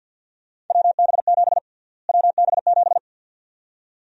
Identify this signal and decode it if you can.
Morse code, heard as a keyed tone, about 52 wpm, 700 Hz. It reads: WB7 WB7
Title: USA Call Sign Prefixes - Repeated 2x - 50wpm